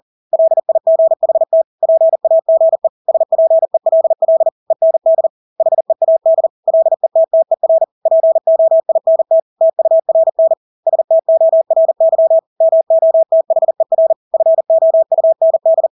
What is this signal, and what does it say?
Morse code, heard as a keyed tone, about 40 wpm, 650 Hz. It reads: LIGHT PAGE SPELL END HEAD LETTER POINT TURN STORY MOTHER FOUND
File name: Engelske ord 40-40 wpm